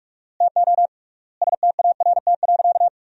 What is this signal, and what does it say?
Morse code, heard as a keyed tone, about 41 wpm, 700 Hz. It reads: TO START.
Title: 3x Word-Spacing - Sentences from Top 300 Words - 45wpm